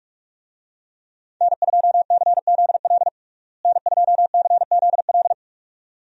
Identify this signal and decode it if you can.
Morse code, heard as a keyed tone, about 46 wpm, 700 Hz. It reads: N2CZL N2CZL
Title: USA Call Signs - 5 Characters - Repeated 2x - 45wpm